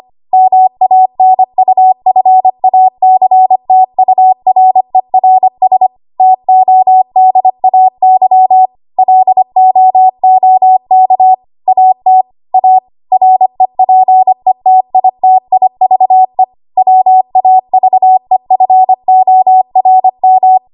Morse code, heard as a keyed tone, about 25 wpm, 750 Hz. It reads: MANUFACTURERS TODAY LOOK AT A REPETITIVE WAVEFORM
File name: Archive / 25 WPM / 25WPM